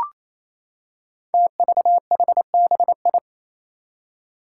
Morse code, heard as a keyed tone, about 28 wpm, 700 Hz. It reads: TVHBI
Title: Five Challenging Letter-Number Combinations 28wpm